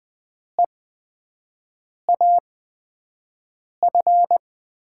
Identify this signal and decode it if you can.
Morse code, heard as a keyed tone, about 20 wpm, spaced as for 5 wpm, 700 Hz. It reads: EAF
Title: USA Call Signs - 6 Characters - 8x Character Spacing - 20wpm